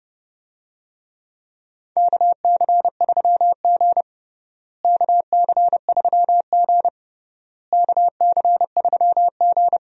Morse code, heard as a keyed tone, about 30 wpm, 700 Hz. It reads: KC3G KC3G KC3G
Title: USA Call Signs - 4 Characters - Repeated 3x - 30wpm